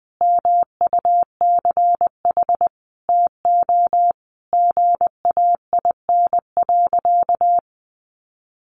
Morse code, heard as a keyed tone, about 20 wpm, 700 Hz. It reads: MUCH TO GAIN.
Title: Sentences from Top 1100 Words 20wpm